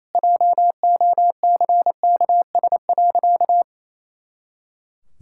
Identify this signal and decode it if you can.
Morse code, heard as a keyed tone, about 28 wpm, 700 Hz. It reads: JOCKS.
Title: Pangram Challenge v2 - 28wpm